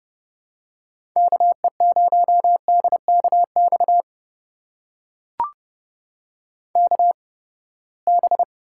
Morse code, heard as a keyed tone, about 32 wpm, spaced as for 21 wpm, 700 Hz. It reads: KE0DKX KB
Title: USA Call Signs - 6 Characters - 8x Character Spacing - 30wpm